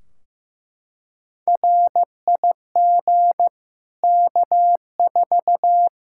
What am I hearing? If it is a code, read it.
Morse code, heard as a keyed tone, about 15 wpm, 700 Hz. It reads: RIG K4
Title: Practice QSO - Exchange 2 - Info - 15wpm